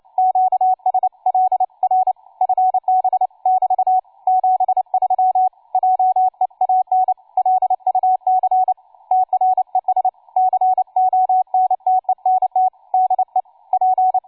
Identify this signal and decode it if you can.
Morse code, heard as a keyed tone, about 29 wpm, 750 Hz, with radio noise. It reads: QSL R FB <BT> 73 JEAN LUC TRES CONTENT DE P
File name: cw forum / tu2 f5ldy 18mhz 16h46